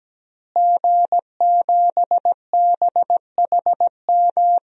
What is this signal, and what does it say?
Morse code, heard as a keyed tone, about 17 wpm, 700 Hz. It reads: G7BHM
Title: DXCC-100 Call Signs - 5 Characters - Repeated 3x - 17wpm